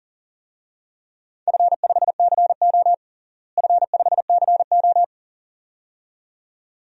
Morse code, heard as a keyed tone, about 40 wpm, 700 Hz. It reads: F5CO F5CO
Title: DXCC-100 Call Signs - 4 Characters - Repeated 2x - 40wpm